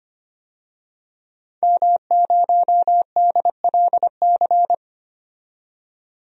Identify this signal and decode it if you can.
Morse code, heard as a keyed tone, about 25 wpm, 700 Hz. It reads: M0DLC
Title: DXCC-100 Call Signs - 5 Characters - 25wpm